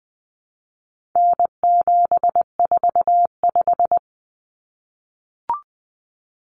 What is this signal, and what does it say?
Morse code, heard as a keyed tone, about 19 wpm, 700 Hz. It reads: N745
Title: Common Cut Numbers - 4-Digit 20wpm